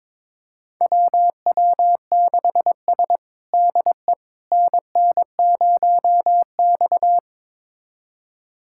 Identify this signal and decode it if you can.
Morse code, heard as a keyed tone, about 22 wpm, 700 Hz. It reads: WW6S DE NN0X
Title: USA Callsign DE Callsign - 4 Characters - 22wpm